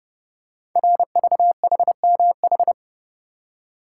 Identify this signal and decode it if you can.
Morse code, heard as a keyed tone, about 31 wpm, 700 Hz. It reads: RVHMH
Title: Code Groups - 5 Letters - 30wpm